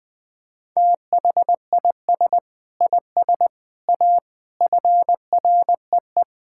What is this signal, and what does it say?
Morse code, heard as a keyed tone, about 20 wpm, 700 Hz. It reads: THIS IS A FREE
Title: Sentences from Top 1000 Words - Review Part 6 - 20wpm